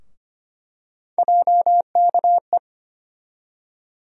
Morse code, heard as a keyed tone, about 25 wpm, 700 Hz. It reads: JKE
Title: USA Call Sign Postfixes - 3 Letters 25wpm